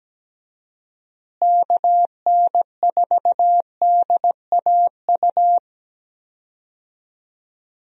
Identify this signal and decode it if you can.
Morse code, heard as a keyed tone, about 17 wpm, 700 Hz. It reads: KN4DAU
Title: USA Call Signs - 6 Characters - Repeated 2x - 17wpm